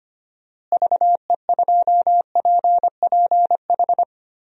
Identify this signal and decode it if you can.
Morse code, heard as a keyed tone, about 25 wpm, 700 Hz. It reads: VE2PPH
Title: DXCC-100 Call Signs - 6 Characters - Repeated 3x - 25wpm